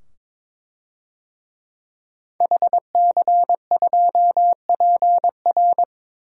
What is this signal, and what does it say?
Morse code, heard as a keyed tone, about 21 wpm, 700 Hz. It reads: HC2PR
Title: DXCC-100 Call Signs - 5 Characters - 22wpm